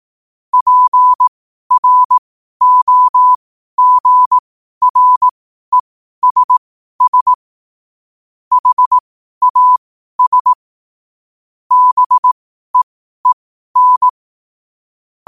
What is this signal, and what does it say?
Morse code, heard as a keyed tone, about 18 wpm, spaced as for 12 wpm, 1000 Hz. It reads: PROGRESS HAS BEEN
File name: qotdfiles / 12wpm